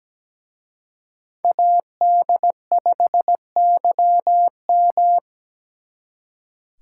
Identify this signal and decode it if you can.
Morse code, heard as a keyed tone, about 21 wpm, 700 Hz. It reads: AD5YM
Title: Speed Racing - USA Call Signs 17wpm